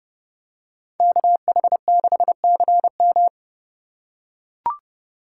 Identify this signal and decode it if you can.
Morse code, heard as a keyed tone, about 29 wpm, 700 Hz. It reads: KH6CM